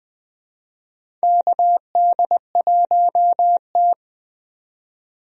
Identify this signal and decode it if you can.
Morse code, heard as a keyed tone, about 20 wpm, 700 Hz. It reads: KD1T
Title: USA Call Signs - 4 Characters - 20wpm